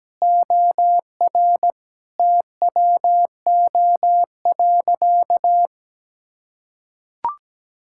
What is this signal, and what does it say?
Morse code, heard as a keyed tone, about 17 wpm, 700 Hz. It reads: OR TWO.
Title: Sentences from Top 1000 Words - Review Part 5 - 17wpm